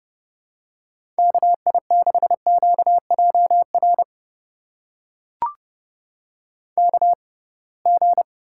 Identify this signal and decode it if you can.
Morse code, heard as a keyed tone, about 31 wpm, spaced as for 23 wpm, 700 Hz. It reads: KI6QJR KG
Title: USA Call Signs - 6 Characters - 6x Character Spacing - 30wpm